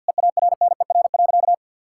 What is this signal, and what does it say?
Morse code, heard as a keyed tone, about 50 wpm, 700 Hz. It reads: EARNER.
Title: Pangram Challenge v2 - 50wpm